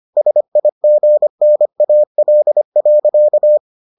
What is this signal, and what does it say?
Morse code, heard as a keyed tone, about 25 wpm, 600 Hz. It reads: SIGNAL.